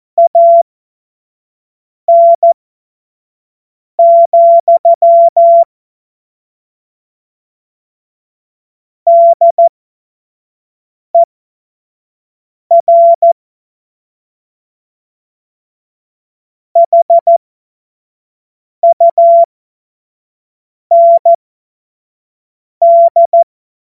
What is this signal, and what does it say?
Morse code, heard as a keyed tone, about 14 wpm, spaced as for 5 wpm, 680 Hz. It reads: AN, DER HUND
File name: Morsekurs / Witze / Witz 20. Juni 2023 25 F70